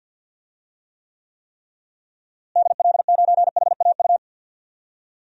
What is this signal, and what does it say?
Morse code, heard as a keyed tone, about 53 wpm, 700 Hz. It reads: DL9HAU
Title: DXCC-100 Call Signs - 6 Characters - Repeated 3x - 50wpm